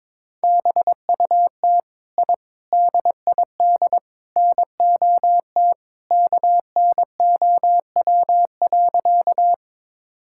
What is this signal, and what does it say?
Morse code, heard as a keyed tone, about 22 wpm, 700 Hz. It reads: BUT I DID NOT KNOW.
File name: practice / 22wpm / Sentences with 4-Word Phrases - 22wpm